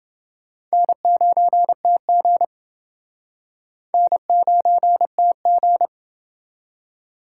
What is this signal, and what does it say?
Morse code, heard as a keyed tone, about 28 wpm, 700 Hz. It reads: N9TG N9TG